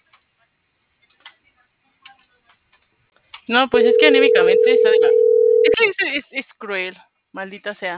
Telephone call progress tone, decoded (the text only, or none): ringback tone